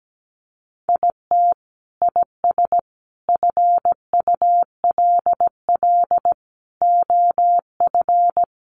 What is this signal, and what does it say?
Morse code, heard as a keyed tone, about 17 wpm, 700 Hz. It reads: IT IS FULL OF
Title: Sentences from Top 1500 Words 17wpm